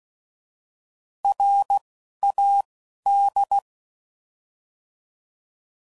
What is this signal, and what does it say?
Morse code, heard as a keyed tone, about 16 wpm, 800 Hz. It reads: R A D
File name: Prefixes / By First Letter / Rs